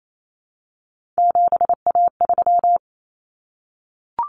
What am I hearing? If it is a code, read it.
Morse code, heard as a keyed tone, about 28 wpm, 700 Hz. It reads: 7A3
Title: Common Cut Numbers - 3-Digit 28wpm